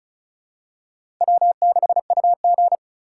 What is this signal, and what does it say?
Morse code, heard as a keyed tone, about 36 wpm, 700 Hz. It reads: W6UG
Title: USA Call Signs - 4 Characters - 2x Character Spacing - 35wpm